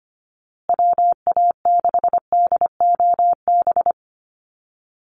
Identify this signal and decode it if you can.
Morse code, heard as a keyed tone, about 25 wpm, 700 Hz. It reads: WA6DOB
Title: USA Call Signs Repeated 2x 25wpm